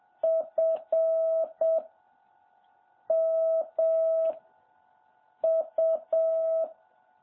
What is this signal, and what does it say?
Morse code, heard as a keyed tone, about 7 wpm, 650 Hz. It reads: F M U
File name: jh8cbh / 2021cw